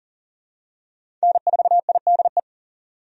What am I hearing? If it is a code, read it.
Morse code, heard as a keyed tone, about 40 wpm, 700 Hz. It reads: N4IDE
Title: USA Call Signs - 5 Characters - 40wpm